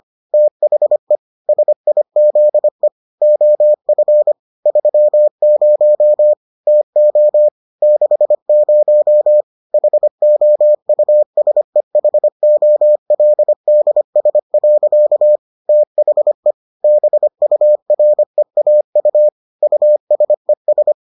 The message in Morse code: THE SIZE OF 30 TO 60 HOUSEHOLDS. THE BUREAU USES

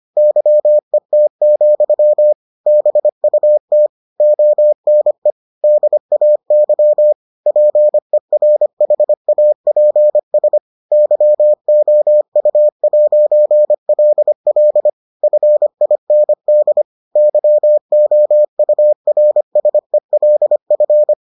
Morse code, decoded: YET, BUT ONE DAY PERHAPS YOU'LL FIND YOURSELF

